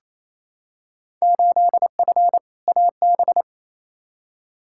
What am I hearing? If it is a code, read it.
Morse code, heard as a keyed tone, about 28 wpm, 700 Hz. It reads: 8F AB
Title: Field Day Exchanges Encore - 28wpm